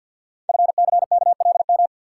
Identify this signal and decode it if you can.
Morse code, heard as a keyed tone, about 50 wpm, 690 Hz. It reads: FCKLM
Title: Code Groups - 5 Letters - 50wpm